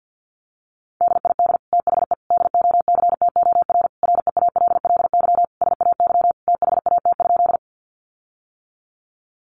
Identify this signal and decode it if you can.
Morse code, heard as a keyed tone, about 50 wpm, 700 Hz. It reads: DID THE DOCTOR REALLY SAY THAT?
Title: Sentences from Top 1500 Words 50wpm